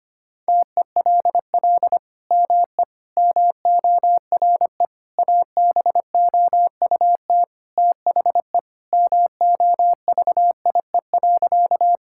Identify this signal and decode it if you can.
Morse code, heard as a keyed tone, about 25 wpm, 700 Hz. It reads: TELL ME MORE ABOUT THE MOVIE.
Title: Sentences from Top 1000 Words - Review Part 5 - 25wpm